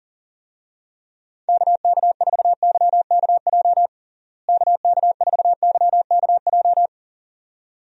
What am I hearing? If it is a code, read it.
Morse code, heard as a keyed tone, about 40 wpm, 700 Hz. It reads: KK4YKJ KK4YKJ
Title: USA Call Signs - 6 Characters - Repeated 2x - 40wpm